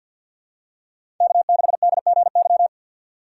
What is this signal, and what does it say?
Morse code, heard as a keyed tone, about 51 wpm, 700 Hz. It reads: K6DGY
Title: USA Call Signs - 5 Characters - 6x Character Spacing - 50wpm